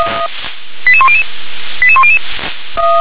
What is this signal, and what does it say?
Five-tone selective call (ZVEI), heard as quick runs of five tones, tones of about 70 ms. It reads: 88100 88100